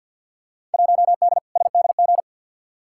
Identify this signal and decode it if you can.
Morse code, heard as a keyed tone, about 52 wpm, 700 Hz. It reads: 1D SDG
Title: Field Day Exchanges - 50wpm